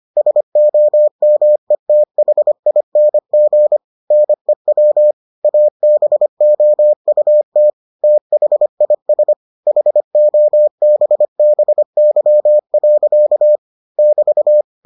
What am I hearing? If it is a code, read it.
Morse code, heard as a keyed tone, about 25 wpm, 600 Hz. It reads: SOMETHING NEW ABOUT THIS HOBBY. <BT>